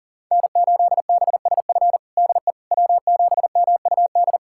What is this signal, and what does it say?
Morse code, heard as a keyed tone, about 40 wpm, 700 Hz. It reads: N8BSF DE W7MUD
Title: USA Callsign DE Callsign - 5 Characters - 40wpm